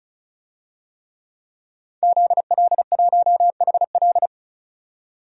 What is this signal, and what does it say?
Morse code, heard as a keyed tone, about 36 wpm, 700 Hz. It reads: ZL1HL
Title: DXCC-100 Call Signs - 5 Characters - 35wpm